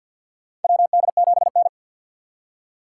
Morse code, heard as a keyed tone, about 48 wpm, 700 Hz. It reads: WD7N
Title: USA Call Signs - 3x Character Spacing - 50wpm